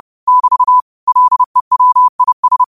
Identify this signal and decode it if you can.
Morse code, heard as a keyed tone, about 30 wpm, 1000 Hz. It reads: <BT> LEWIS